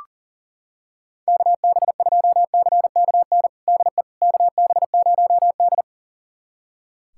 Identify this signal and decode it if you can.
Morse code, heard as a keyed tone, about 40 wpm, 700 Hz. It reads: KB2CKN DE KB0D
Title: USA Callsign DE Callsign 40wpm